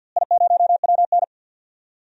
Morse code, heard as a keyed tone, about 52 wpm, 700 Hz. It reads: I0WN
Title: DXCC-100 Call Signs - 4 Characters - Repeated 3x - 50wpm